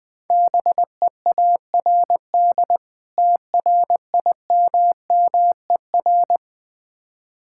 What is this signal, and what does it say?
Morse code, heard as a keyed tone, about 20 wpm, 700 Hz. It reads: BEARD TRIMMER
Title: Difficult Words - Speed-Racing with Hints - 20wpm